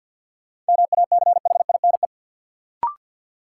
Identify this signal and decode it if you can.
Morse code, heard as a keyed tone, about 53 wpm, 700 Hz. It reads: MACHINE
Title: Sentences from Top 1000 Words - Review Part 2 - New Word Drill - 50wpm